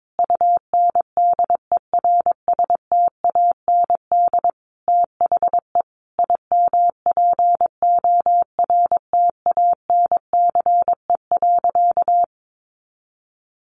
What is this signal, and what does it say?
Morse code, heard as a keyed tone, about 22 wpm, 700 Hz. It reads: UNDERSTAND THE IMPORTANCE.
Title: Sentences from Top 1200 Words 22wpm